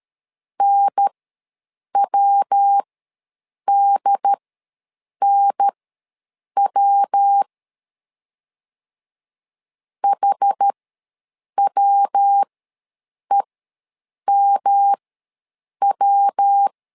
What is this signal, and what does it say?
Morse code, heard as a keyed tone, about 13 wpm, spaced as for 7 wpm, 800 Hz. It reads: NWDNW HWEMW
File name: LEZIONI / 3 MWRO / 3 MW ws 12wpm 1816